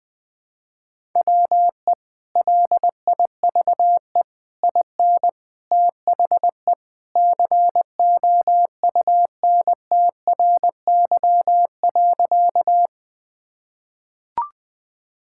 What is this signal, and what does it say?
Morse code, heard as a keyed tone, about 20 wpm, 700 Hz. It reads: WE LIVE IN THE COUNTRY.
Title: Sentences with 3-Word Phrases - 20wpm